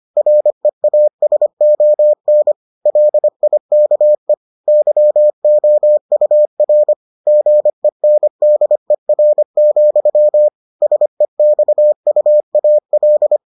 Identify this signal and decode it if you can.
Morse code, heard as a keyed tone, about 25 wpm, 600 Hz. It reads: REASON LIKE YOUR GENDER, SEXUAL